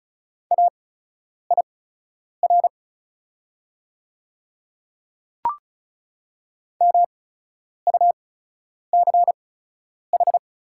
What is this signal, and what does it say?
Morse code, heard as a keyed tone, about 36 wpm, spaced as for 10 wpm, 690 Hz. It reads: AIR MUCH
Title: 8x Character Spacing - Top 300 Words - 35wpm